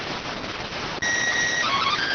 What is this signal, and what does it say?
SSTV (slow-scan television), PD 290.